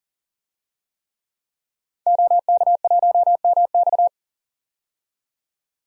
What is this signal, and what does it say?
Morse code, heard as a keyed tone, about 40 wpm, 700 Hz. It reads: OK1MX
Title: DXCC-100 Call Signs - 5 Characters - 40wpm